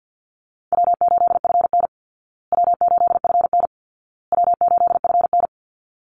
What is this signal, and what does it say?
Morse code, heard as a keyed tone, about 50 wpm, 700 Hz. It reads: W8FN W8FN W8FN
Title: Easy Call Signs Repeated 3x - 50wpm